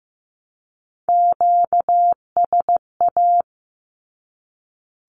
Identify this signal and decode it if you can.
Morse code, heard as a keyed tone, about 15 wpm, 700 Hz. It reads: QSA
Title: Q-Codes 15wpm